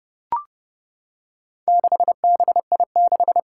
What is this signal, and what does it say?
Morse code, heard as a keyed tone, about 30 wpm, 700 Hz. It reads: 6BI6
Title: Four Challenging Letter-Number Combinations 30wpm